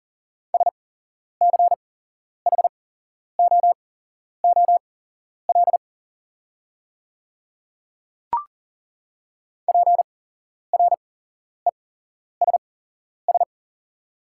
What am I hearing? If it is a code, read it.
Morse code, heard as a keyed tone, about 40 wpm, spaced as for 11 wpm, 700 Hz. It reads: SCHOOL PRESS
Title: 8x Character Spacing - Top 300 Words - 40wpm